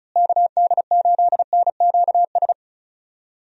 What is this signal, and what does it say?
Morse code, heard as a keyed tone, about 36 wpm, 700 Hz. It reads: KD8NQS
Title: USA Call Signs - 6 Characters - Repeated 3x - 35wpm